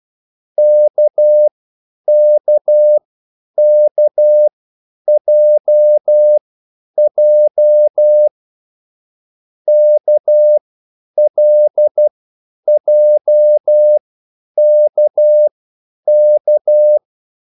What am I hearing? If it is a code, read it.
Morse code, heard as a keyed tone, about 12 wpm, spaced as for 9 wpm, 600 Hz. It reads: KKKJJ KLJKK
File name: QSL / CW lesson